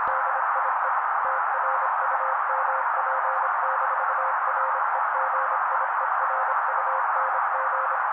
Morse code, heard as a keyed tone, about 27 wpm, 530 Hz. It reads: DIE TRUMP-REGIERUNG
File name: daily / 25wpm noise